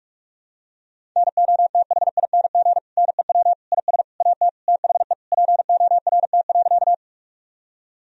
Morse code, heard as a keyed tone, about 46 wpm, 700 Hz. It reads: NOTHING NEW IS AT THE PORT.
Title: Sentences from Top 1000 Words - Review Part 1 - 45wpm